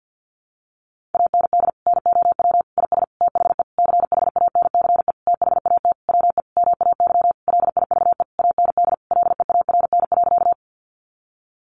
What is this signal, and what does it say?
Morse code, heard as a keyed tone, about 50 wpm, 700 Hz. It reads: AND NOW IS THE CHANCE THAT WE MAY LIVE AND LEARN.
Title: Sentences from Top 800 Words 50wpm